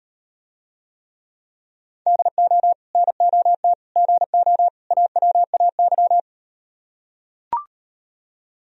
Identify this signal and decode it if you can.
Morse code, heard as a keyed tone, about 38 wpm, 700 Hz. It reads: DO NOT GO AWAY